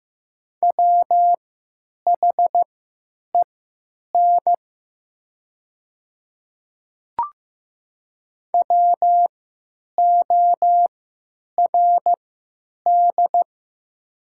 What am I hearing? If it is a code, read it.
Morse code, heard as a keyed tone, about 15 wpm, spaced as for 9 wpm, 700 Hz. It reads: WHEN WORD
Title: 3x Character Spacing - Top 100 Words - 15wpm